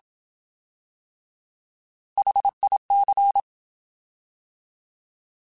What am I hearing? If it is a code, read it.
Morse code, heard as a keyed tone, about 27 wpm, 800 Hz. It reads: HIC
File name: Alphabetical / Alphabetical mp3s / Hs @25